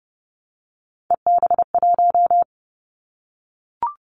Morse code, heard as a keyed tone, about 30 wpm, 700 Hz. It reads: EB1
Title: Callsign Prefixes for Top 25 DXCC - 30wpm